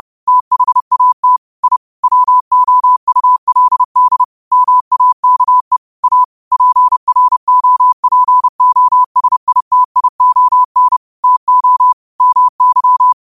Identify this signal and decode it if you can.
Morse code, heard as a keyed tone, about 30 wpm, 1000 Hz. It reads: THAT I WOULD MAKE A PROPOSITION TO MY
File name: qotdfiles / 30wpm